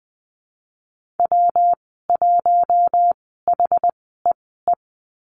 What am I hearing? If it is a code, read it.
Morse code, heard as a keyed tone, about 19 wpm, spaced as for 14 wpm, 700 Hz. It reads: W1HEE
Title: 128 Lesson 21g - Number 6 - Call Signs 20wpm